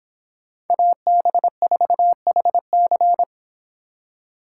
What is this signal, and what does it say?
Morse code, heard as a keyed tone, about 26 wpm, 700 Hz. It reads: AB4HC